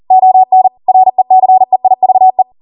Morse code, heard as a keyed tone, about 40 wpm, 750 Hz. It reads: ON RECEIVE
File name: Morse / 40WPM